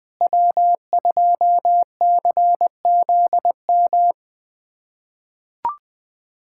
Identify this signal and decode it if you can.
Morse code, heard as a keyed tone, about 20 wpm, 700 Hz. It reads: W2CZM
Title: USA Callsign DE Callsign 20wpm